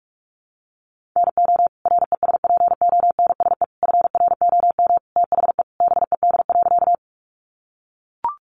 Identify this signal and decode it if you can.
Morse code, heard as a keyed tone, about 44 wpm, 700 Hz. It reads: NO RESPONSE FROM THE BED.
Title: Sentences from Top 1400 Words 45wpm